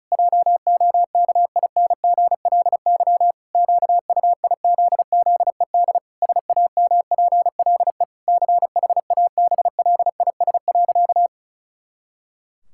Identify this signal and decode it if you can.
Morse code, heard as a keyed tone, about 34 wpm, 700 Hz. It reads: JOKINGLY QUIZZED SAMPLE CHABLIS.
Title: Pangram Challenge v2 - 35wpm